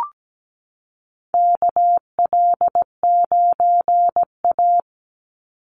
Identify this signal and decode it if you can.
Morse code, heard as a keyed tone, about 17 wpm, 700 Hz. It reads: KL9A
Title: Call Signs - 17wpm